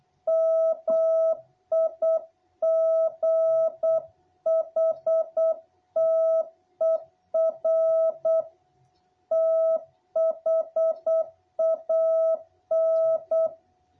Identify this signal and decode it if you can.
Morse code, heard as a keyed tone, about 8 wpm, 660 Hz. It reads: MIGHTER THAN
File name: jh8cbh / 2021cw